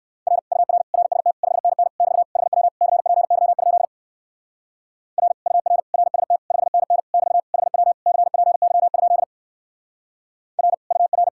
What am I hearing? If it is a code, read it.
Morse code, heard as a keyed tone, about 62 wpm, 700 Hz. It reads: R UR RST 5NN <BT> HW CPY? R UR RST 5NN <BT> HW CPY? R UR